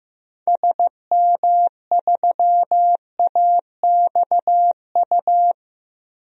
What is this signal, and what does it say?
Morse code, heard as a keyed tone, about 15 wpm, 700 Hz. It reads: SM3AXU
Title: DXCC-100 Call Signs - 6 Characters - Repeated 3x - 15wpm